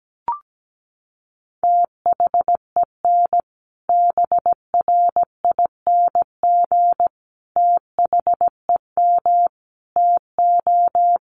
Morse code, heard as a keyed tone, about 17 wpm, 700 Hz. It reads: THEN BRING THEM TO
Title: Sentences from Top 500 Words 17wpm